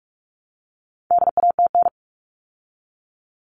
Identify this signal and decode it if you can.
Morse code, heard as a keyed tone, about 44 wpm, 700 Hz. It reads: DATN
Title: All Cut Numbers - 4-Digit 45wpm